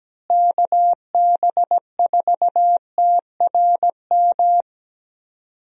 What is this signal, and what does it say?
Morse code, heard as a keyed tone, about 17 wpm, 700 Hz. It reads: KB4TRM
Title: USA Call Signs - 6 Characters - Repeated 2x - 17wpm